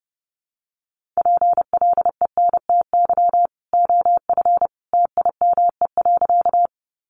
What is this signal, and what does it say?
Morse code, heard as a keyed tone, about 30 wpm, 700 Hz. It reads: PLENTY OF TIME.
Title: Sentences from Top 1300 Words 30wpm